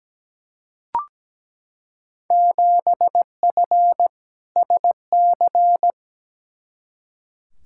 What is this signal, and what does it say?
Morse code, heard as a keyed tone, about 17 wpm, 700 Hz. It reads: 7F SC
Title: Field Day Exchanges Encore - 17wpm